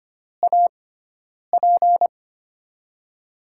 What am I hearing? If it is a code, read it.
Morse code, heard as a keyed tone, about 25 wpm, spaced as for 9 wpm, 700 Hz. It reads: AP
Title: USA Call Signs - 6x Character Spacing - 25wpm